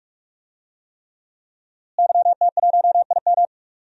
Morse code, heard as a keyed tone, about 41 wpm, 700 Hz. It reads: YT1IM
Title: DXCC-100 Call Signs - 5 Characters - Repeated 3x - 45wpm